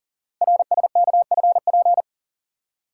Morse code, heard as a keyed tone, about 44 wpm, 700 Hz. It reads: RSKFP
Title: Code Groups - 5 Letters - 40wpm